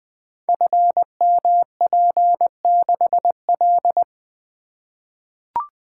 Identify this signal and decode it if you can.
Morse code, heard as a keyed tone, about 20 wpm, 700 Hz. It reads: FMP6L
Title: Code Groups - 5 Letter:Numbers - 20wpm